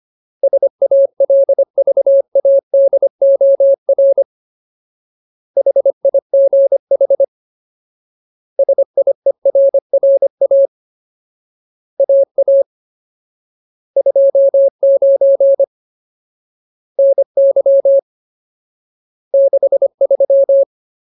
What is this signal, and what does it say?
Morse code, heard as a keyed tone, about 25 wpm, 550 Hz. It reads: SALVADOR HIGH SIERRA AA 29 NY 63
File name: Ultimate Word List / 25WPM Ultimate